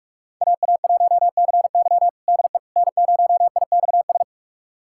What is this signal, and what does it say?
Morse code, heard as a keyed tone, about 46 wpm, 700 Hz. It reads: AA1CY DE N0IXS
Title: USA Callsign DE Callsign - 5 Characters - 45wpm